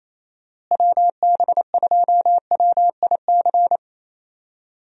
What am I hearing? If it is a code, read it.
Morse code, heard as a keyed tone, about 28 wpm, 700 Hz. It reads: WB2WIC
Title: USA Call Signs - 6 Characters - Repeated 3x - 28wpm